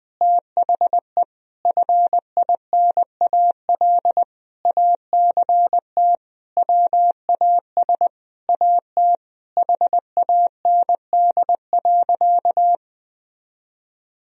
Morse code, heard as a keyed tone, about 20 wpm, 700 Hz. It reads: THE FINAL ACT WAS AT HAND.
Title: Sentences from Top 1000 Words - Review Part 2 - 20wpm